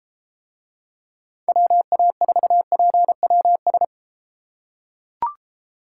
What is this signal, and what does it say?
Morse code, heard as a keyed tone, about 33 wpm, 700 Hz. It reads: WA4PWS